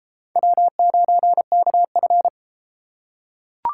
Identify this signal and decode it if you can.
Morse code, heard as a keyed tone, about 30 wpm, 700 Hz. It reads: W9KF